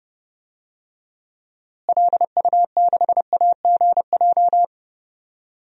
Morse code, heard as a keyed tone, about 32 wpm, 700 Hz. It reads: LU6AGJ
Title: DXCC-100 Call Signs - 6 Characters - Repeated 3x - 30wpm